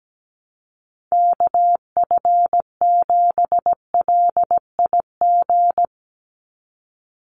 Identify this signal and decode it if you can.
Morse code, heard as a keyed tone, about 17 wpm, 700 Hz. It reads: KF7LIG
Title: USA Call Signs 17wpm